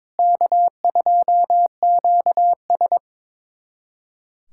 Morse code, heard as a keyed tone, about 22 wpm, 700 Hz. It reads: K2QS
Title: USA Callsign DE Callsign - 4 Characters - 22wpm